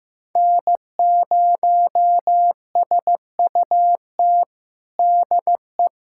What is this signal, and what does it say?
Morse code, heard as a keyed tone, about 15 wpm, 700 Hz. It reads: N0SUT DE
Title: USA Callsign DE Callsign - 5 Characters - 15wpm